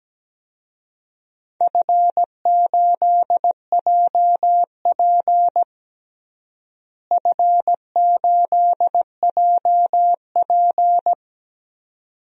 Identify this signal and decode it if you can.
Morse code, heard as a keyed tone, about 17 wpm, 700 Hz. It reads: F8JP F8JP
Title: DXCC-100 Call Signs - 4 Characters - Repeated 2x - 17wpm